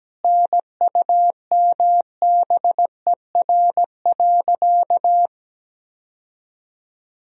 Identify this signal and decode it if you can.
Morse code, heard as a keyed tone, about 17 wpm, 700 Hz. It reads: NUMBER.
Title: Sentences from Top 1000 Words - Review Part 1 - 17wpm